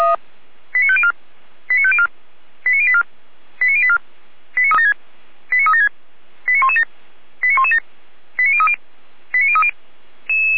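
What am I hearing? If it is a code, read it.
Five-tone selective call (ZVEI), heard as quick runs of five tones, tones of about 70 ms. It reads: 89504 89504 89084 89084 89376 89117 89117 89030 89030
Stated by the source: Decoded by SDR as ZVEI five-tone selective call.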